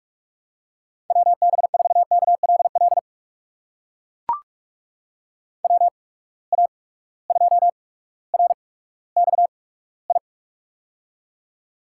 Morse code, heard as a keyed tone, about 44 wpm, spaced as for 19 wpm, 700 Hz. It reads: WB4KLL WA2RXI
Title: USA Call Signs - 6 Characters - 8x Character Spacing - 45wpm